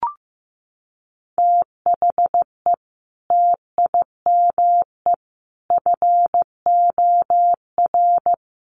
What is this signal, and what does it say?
Morse code, heard as a keyed tone, about 15 wpm, 700 Hz. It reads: THE TIME FOR
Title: Sentences from Top 700 Words 15wpm